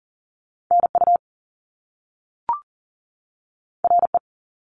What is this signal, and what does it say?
Morse code, heard as a keyed tone, about 38 wpm, 700 Hz. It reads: NU RE
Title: Top Two Letter Combinations in Words 40wpm